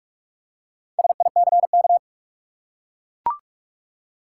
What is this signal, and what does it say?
Morse code, heard as a keyed tone, about 42 wpm, 700 Hz. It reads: SICK
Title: Four Letter Words - 45wpm